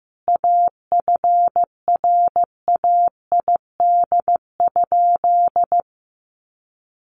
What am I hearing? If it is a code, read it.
Morse code, heard as a keyed tone, about 15 wpm, 700 Hz. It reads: AFRAID?
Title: Sentences from Top 1500 Words 15wpm